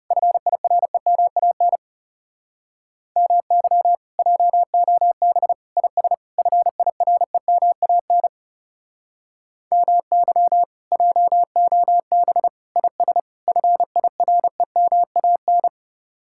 Morse code, heard as a keyed tone, about 40 wpm, 700 Hz. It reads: FIREMAN MY JOB IS FIREMAN MY JOB IS FIREMAN